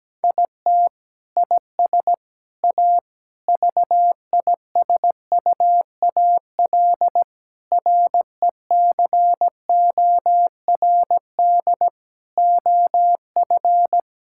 Morse code, heard as a keyed tone, about 17 wpm, 700 Hz. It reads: IT IS A VISUAL RECORD OF
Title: Sentences from Top 1000 Words - Review Part 6 - 17wpm